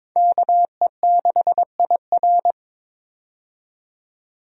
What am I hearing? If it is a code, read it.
Morse code, heard as a keyed tone, about 21 wpm, 700 Hz. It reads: KE6IR
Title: USA Call Signs - 5 Characters - 22wpm